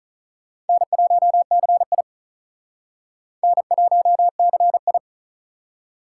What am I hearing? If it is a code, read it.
Morse code, heard as a keyed tone, about 45 wpm, 700 Hz. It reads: N1CI N1CI